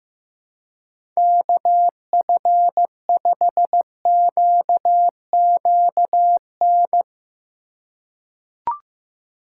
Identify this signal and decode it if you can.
Morse code, heard as a keyed tone, about 15 wpm, 700 Hz. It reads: KF5QQN
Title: USA Call Signs - 6 Characters - 2x Character Spacing - 15wpm